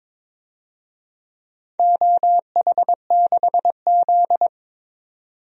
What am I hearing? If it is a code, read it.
Morse code, heard as a keyed tone, about 22 wpm, 700 Hz. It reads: OH6Z
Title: DXCC-100 Call Signs - 4 Characters - 22wpm